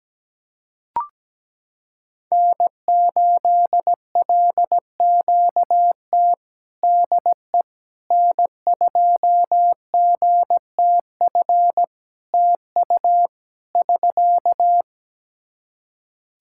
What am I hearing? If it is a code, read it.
Morse code, heard as a keyed tone, about 17 wpm, 700 Hz. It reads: N8LQT DE N2GTF TU <SK>
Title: K1USN SST QSO Practice 17wpm